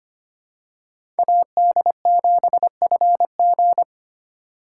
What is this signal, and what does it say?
Morse code, heard as a keyed tone, about 25 wpm, 700 Hz. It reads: AD7FG
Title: USA Call Signs - 4x Character Spacing - 25wpm